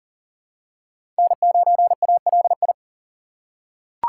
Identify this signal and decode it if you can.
Morse code, heard as a keyed tone, about 40 wpm, 700 Hz. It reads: N9ALI
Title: USA Call Signs - 5 Characters - 3x Character Spacing - 40wpm